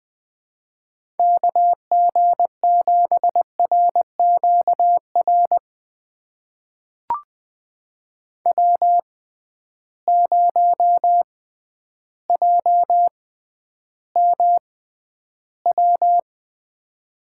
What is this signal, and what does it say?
Morse code, heard as a keyed tone, about 20 wpm, spaced as for 11 wpm, 700 Hz. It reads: KG7RQR W0JMW
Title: USA Call Signs - 6x Character Spacing - 20wpm